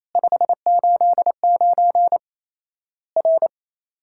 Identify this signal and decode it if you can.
Morse code, heard as a keyed tone, about 28 wpm, 700 Hz. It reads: 589 R
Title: Worked All Europe DX Contest - QTC Exchange - 28wpm